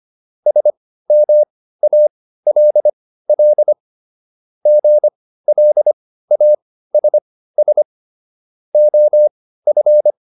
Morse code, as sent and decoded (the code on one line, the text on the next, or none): ... -- .- .-.. .-.. / --. .-.. .- ... ... / --- ..-.
SMALL GLASS OF